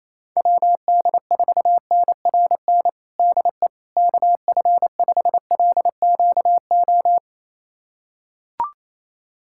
Morse code, heard as a keyed tone, about 27 wpm, 700 Hz. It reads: WD4NRN DE KF5LQO
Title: USA Callsign DE Callsign - 6 Characters - 28wpm